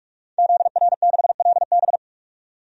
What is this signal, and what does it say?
Morse code, heard as a keyed tone, about 42 wpm, 700 Hz. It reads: ZR6LB
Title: DXCC-100 Call Signs - 5 Characters - 45wpm